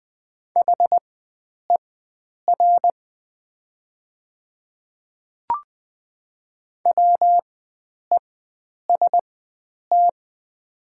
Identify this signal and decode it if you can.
Morse code, heard as a keyed tone, about 20 wpm, spaced as for 9 wpm, 700 Hz. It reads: HER WEST
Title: 4x Character Spacing - Top 500 Words - 20wpm